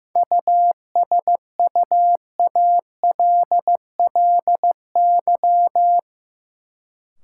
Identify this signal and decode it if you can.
Morse code, heard as a keyed tone, about 15 wpm, 700 Hz. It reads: USUALLY
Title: Sentences from Top 1000 Words - Review Part 3 - New Word Drill - 15wpm